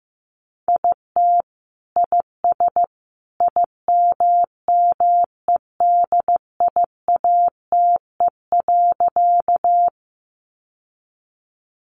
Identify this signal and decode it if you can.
Morse code, heard as a keyed tone, about 15 wpm, 700 Hz. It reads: IT IS IMMEDIATE.
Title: Sentences from Top 1200 Words 15wpm